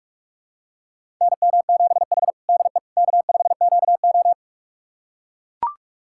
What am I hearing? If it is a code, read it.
Morse code, heard as a keyed tone, about 44 wpm, 690 Hz. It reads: NM7H DE K5QO
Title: USA Callsign DE Callsign - 4 Characters - 45wpm